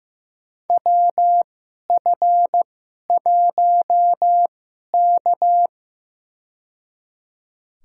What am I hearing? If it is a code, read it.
Morse code, heard as a keyed tone, about 15 wpm, spaced as for 11 wpm, 700 Hz. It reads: WF1K
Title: USA Call Signs - 4 Characters - 2x Character Spacing - 15wpm